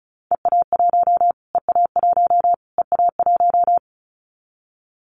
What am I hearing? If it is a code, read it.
Morse code, heard as a keyed tone, about 34 wpm, 700 Hz. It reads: EA1 EA1 EA1
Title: Callsign Prefixes for Top 25 DXCC - Repeated 3x - 35wpm